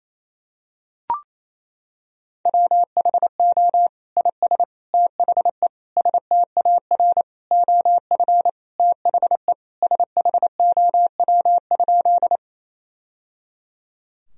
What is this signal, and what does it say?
Morse code, heard as a keyed tone, about 28 wpm, 700 Hz. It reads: WHO IS THE STAR OF THE SHOW?
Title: Sentences from Top 1000 Words - Review Part 2 - 28wpm